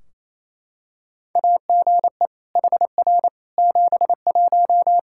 Morse code, heard as a keyed tone, about 28 wpm, 700 Hz. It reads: AGE HR 71
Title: Practice QSO - Exchange 3 - More Info - 28wpm